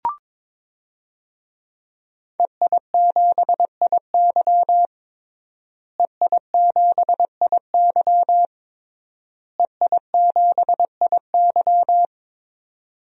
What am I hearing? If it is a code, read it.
Morse code, heard as a keyed tone, about 22 wpm, 700 Hz. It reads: EI7IY EI7IY EI7IY
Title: DXCC-100 Call Signs - 5 Characters - Repeated 3x - 22wpm